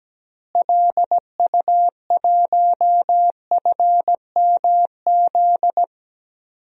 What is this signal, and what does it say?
Morse code, heard as a keyed tone, about 17 wpm, 700 Hz. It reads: LU1FMZ
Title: DXCC-100 Call Signs - 6 Characters - Repeated 3x - 17wpm